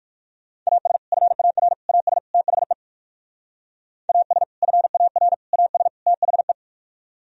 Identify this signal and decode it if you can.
Morse code, heard as a keyed tone, about 59 wpm, 700 Hz. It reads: AS FAR AS THE AS FAR AS THE